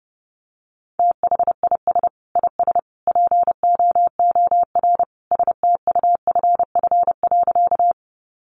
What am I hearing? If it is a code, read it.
Morse code, heard as a keyed tone, about 30 wpm, 700 Hz. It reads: THIS IS POOR STUFF.
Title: Sentences from Top 1400 Words 30wpm